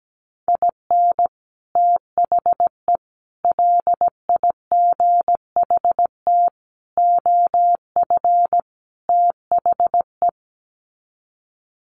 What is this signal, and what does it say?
Morse code, heard as a keyed tone, about 17 wpm, 700 Hz. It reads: IN THE LIGHT OF THE
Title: Sets of 5 Words Encore2 17wpm